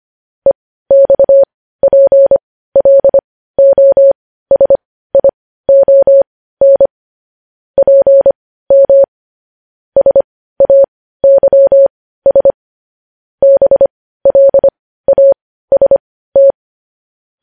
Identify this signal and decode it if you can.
Morse code, heard as a keyed tone, about 25 wpm, spaced as for 15 wpm, 550 Hz. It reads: EXPLOSION PM SAYS BLAST
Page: News Headlines in Morse Code at 15 WPM